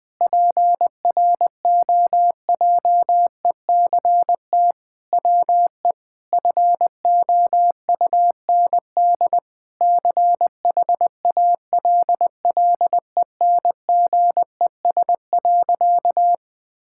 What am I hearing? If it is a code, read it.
Morse code, heard as a keyed tone, about 21 wpm, 700 Hz. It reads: PROJECT WE FOUND CHALLENGES.
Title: Sentences with 6-Word Phrases - 20wpm